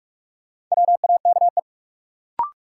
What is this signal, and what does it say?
Morse code, heard as a keyed tone, about 46 wpm, 700 Hz. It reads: WAKE
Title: Sets of 4 Words - New Word Drill - 45wpm